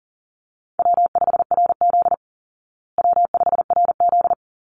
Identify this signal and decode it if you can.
Morse code, heard as a keyed tone, about 40 wpm, 700 Hz. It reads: W5RZ W5RZ
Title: Call Signs Repeated 3x - 40wpm